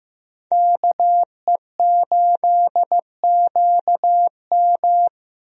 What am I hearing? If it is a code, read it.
Morse code, heard as a keyed tone, about 15 wpm, 700 Hz. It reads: KE8QM
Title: USA Call Signs - 5 Characters - Repeated 3x - 15wpm